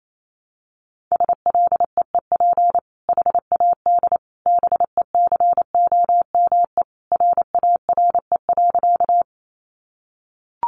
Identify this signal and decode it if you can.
Morse code, heard as a keyed tone, about 28 wpm, 700 Hz. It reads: SLEEP HAD BECOME RARE.
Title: Sentences from Top 1100 Words 28wpm